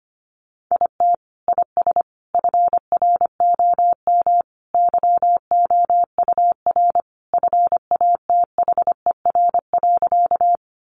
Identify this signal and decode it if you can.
Morse code, heard as a keyed tone, about 25 wpm, 700 Hz. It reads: IT IS FROM YOUR FATHER.
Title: Sentences from Top 300 Words 25wpm